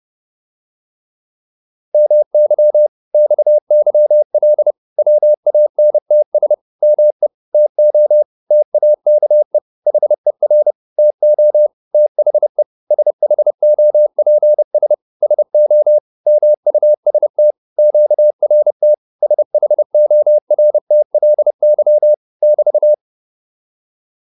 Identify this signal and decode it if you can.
Morse code, heard as a keyed tone, about 30 wpm, 600 Hz. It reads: MY XYL WANTS ME TO TAKE HER TO THE SHOPS SO MUST QRT SHORTLY <BT>
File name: QSO406 30